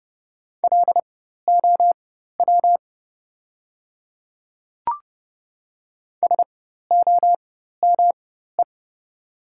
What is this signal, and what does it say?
Morse code, heard as a keyed tone, about 30 wpm, spaced as for 14 wpm, 700 Hz. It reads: LOW SOME
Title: 4x Character Spacing - Top 200 Words - 30wpm